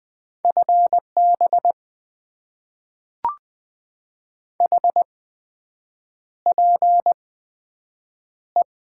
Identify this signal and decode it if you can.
Morse code, heard as a keyed tone, about 20 wpm, spaced as for 7 wpm, 700 Hz. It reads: FB HPE
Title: Top 90 QSO Elements - 8x Character Spacing - 20wpm